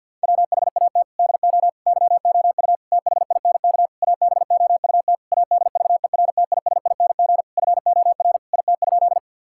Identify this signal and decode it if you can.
Morse code, heard as a keyed tone, about 52 wpm, 700 Hz. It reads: WHAT DO YOU THINK ABOUT ADVERTISING FOR IT?
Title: Sentences from Top 1000 Words - Review Part 6 - 50wpm